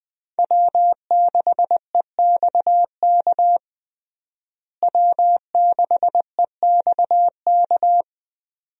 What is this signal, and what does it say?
Morse code, heard as a keyed tone, about 20 wpm, 700 Hz. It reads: W6EXK W6EXK
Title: USA Call Signs - 5 Characters - Repeated 2x - 20wpm